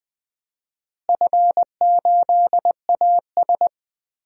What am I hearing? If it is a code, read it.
Morse code, heard as a keyed tone, about 20 wpm, 700 Hz. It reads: F8AS
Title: DXCC-100 Call Signs - 4 Characters - Repeated 3x - 20wpm